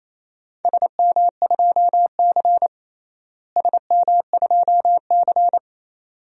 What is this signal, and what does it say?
Morse code, heard as a keyed tone, about 28 wpm, 700 Hz. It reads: SM2C SM2C
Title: DXCC-100 Call Signs - 4 Characters - Repeated 2x - 28wpm